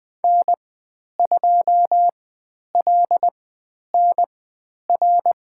Morse code, heard as a keyed tone, about 20 wpm, spaced as for 10 wpm, 700 Hz. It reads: N2LNR
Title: 109 Lesson 18g - Letter m - Call Signs 20wpm